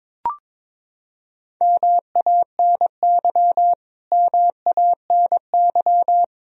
Morse code, heard as a keyed tone, about 22 wpm, 700 Hz. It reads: MANY MANY
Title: Top 100 Words Repeated 22wpm